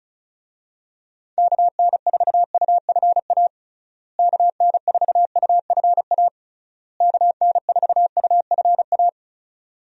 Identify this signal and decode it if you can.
Morse code, heard as a keyed tone, about 36 wpm, 700 Hz. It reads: KN4UFA KN4UFA KN4UFA
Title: USA Call Signs - 6 Characters - Repeated 3x - 35wpm